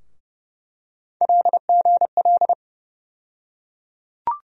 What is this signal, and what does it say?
Morse code, heard as a keyed tone, about 30 wpm, 700 Hz. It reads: LGL
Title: USA Call Sign Postfixes - 3 Letters 30wpm